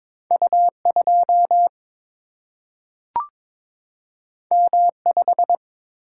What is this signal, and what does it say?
Morse code, heard as a keyed tone, about 22 wpm, 700 Hz. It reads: U 2 M 5
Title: Single-Letter Single-Number 22wpm